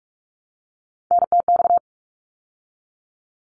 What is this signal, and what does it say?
Morse code, heard as a keyed tone, about 44 wpm, 700 Hz. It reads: NTX
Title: ARRL Sections 45wpm